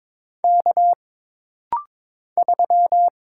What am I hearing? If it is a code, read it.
Morse code, heard as a keyed tone, about 22 wpm, 700 Hz. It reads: K 3
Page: This rapid-fire timing is geared to push your ICR (Instant Character Recognition) skills.